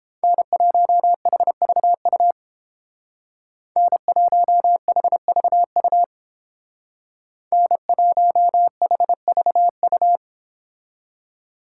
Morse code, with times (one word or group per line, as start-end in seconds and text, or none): 0.2-2.3 N1HVU
3.8-6.0 N1HVU
7.5-10.2 N1HVU